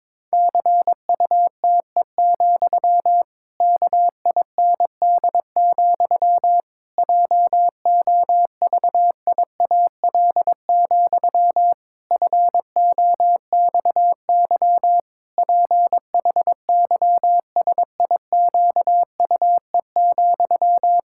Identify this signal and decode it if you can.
Morse code, heard as a keyed tone, about 22 wpm, 700 Hz. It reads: CUTE, KIND, JOVIAL, FOXY PHYSIQUE,
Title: Pangram Challenge v2 - 22wpm